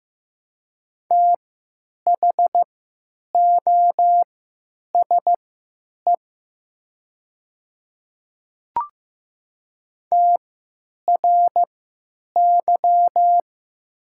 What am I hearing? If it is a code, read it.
Morse code, heard as a keyed tone, about 15 wpm, spaced as for 9 wpm, 700 Hz. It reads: THOSE TRY
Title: 3x Character Spacing - Top 300 Words - 15wpm